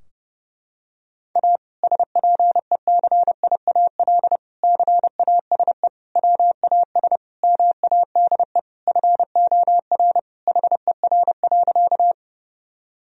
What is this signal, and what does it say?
Morse code, heard as a keyed tone, about 30 wpm, 700 Hz. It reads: A SPECIAL CASE WAS MADE FOR HER.
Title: Sentences from Top 1000 Words - Review Part 3 - 30wpm